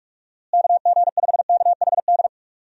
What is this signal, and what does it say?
Morse code, heard as a keyed tone, about 43 wpm, 700 Hz. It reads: KG5KHD
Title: USA Call Signs - 6 Characters - Repeated 3x - 45wpm